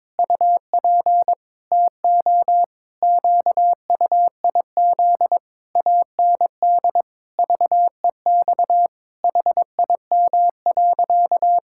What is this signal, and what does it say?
Morse code, heard as a keyed tone, about 22 wpm, 700 Hz. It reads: UP TO QUIZ AND VEX HIM.
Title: Pangram Challenge v2 - 22wpm